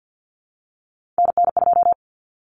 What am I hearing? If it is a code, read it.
Morse code, heard as a keyed tone, about 48 wpm, 700 Hz. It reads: NN2
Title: Common Cut Numbers - 3-Digit 50wpm